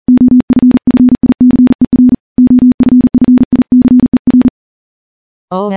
Morse code, heard as a keyed tone, about 40 wpm, 250 Hz. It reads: OFFICER OFFICER
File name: morse-training-40WPM